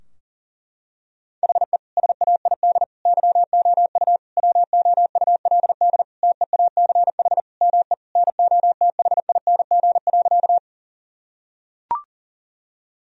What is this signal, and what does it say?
Morse code, heard as a keyed tone, about 40 wpm, 700 Hz. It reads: HE SAID YOU WOULD TEACH ME NOTHING.
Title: Sentences from Top 1000 Words - Review Part 2 - 40wpm